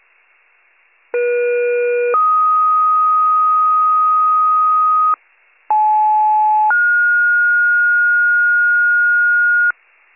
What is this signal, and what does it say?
Two-tone page, heard as a one-second tone, then a three-second tone, about 480 then 1250 Hz, about 860 then 1430 Hz. It is heard from 1.1 to 9.7 s.